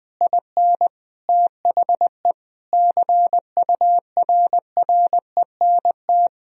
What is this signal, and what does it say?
Morse code, heard as a keyed tone, about 20 wpm, 700 Hz. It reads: IN THE CURRENT
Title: Sentences from Top 1000 Words - Review Part 4 - 20wpm